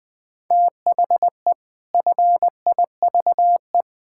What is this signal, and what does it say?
Morse code, heard as a keyed tone, about 20 wpm, 700 Hz. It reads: THE FIVE
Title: Pangram Challenge v2 - 20wpm